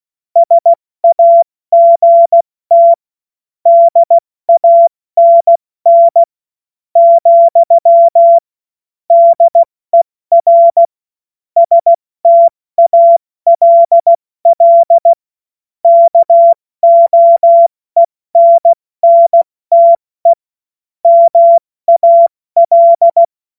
Morse code, heard as a keyed tone, about 16 wpm, spaced as for 14 wpm, 680 Hz. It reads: SAGT DANN, DER STALL KOENNTE MAL
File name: Witz-16-September-2025-70-F80